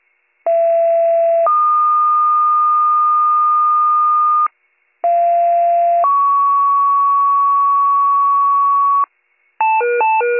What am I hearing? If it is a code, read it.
Two-tone page, heard as a one-second tone, then a three-second tone, about 670 then 1220 Hz, about 690 then 1120 Hz. It is heard from 0.5 to 9.0 s.